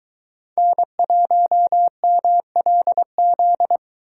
Morse code, heard as a keyed tone, about 22 wpm, 700 Hz. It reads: N1MLZ